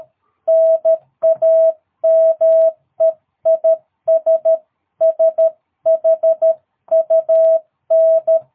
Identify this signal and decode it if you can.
Morse code, heard as a keyed tone, about 13 wpm, 660 Hz. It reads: NAME IS SHUN
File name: jh8cbh / 2022cw